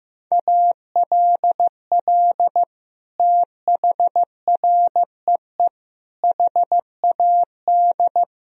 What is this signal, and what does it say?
Morse code, heard as a keyed tone, about 15 wpm, 700 Hz. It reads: ALL THREE HAD
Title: Sentences from Top 1000 Words - Review Part 2 - 15wpm